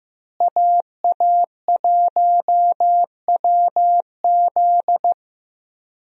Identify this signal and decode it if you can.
Morse code, heard as a keyed tone, about 15 wpm, 700 Hz. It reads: AA1WZ
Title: USA Call Signs - 5 Characters - Repeated 2x - 15wpm